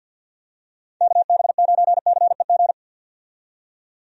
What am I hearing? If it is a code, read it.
Morse code, heard as a keyed tone, about 51 wpm, 700 Hz. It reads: KB9CEG
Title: USA Call Signs - 6 Characters - 50wpm